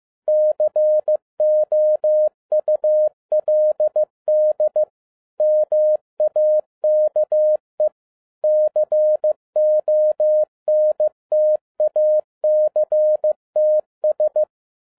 Morse code, heard as a keyed tone, about 15 wpm, 610 Hz. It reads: COULD MAKE CONTACTS
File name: EXCITING QRP CW OPERATION 15wpm